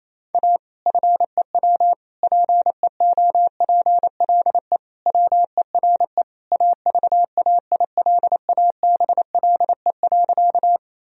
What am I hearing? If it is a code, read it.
Morse code, heard as a keyed tone, about 27 wpm, 700 Hz. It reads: A FEW PEOPLE WERE AVAILABLE.
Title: Sentences from Top 1000 Words - Review Part 3 - 28wpm